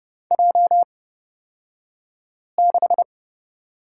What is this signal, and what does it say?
Morse code, heard as a keyed tone, about 30 wpm, spaced as for 5 wpm, 700 Hz. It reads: J6
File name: Lesson 38d - Letter q - Two Characters 30wpm